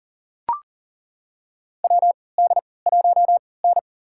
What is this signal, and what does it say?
Morse code, heard as a keyed tone, about 37 wpm, spaced as for 22 wpm, 700 Hz. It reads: WD1N
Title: USA Call Signs - 4 Characters - 3x Character Spacing - 40wpm